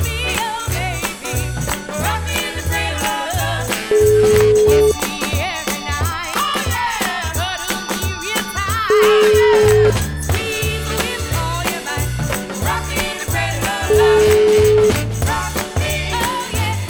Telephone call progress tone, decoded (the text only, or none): ringback tone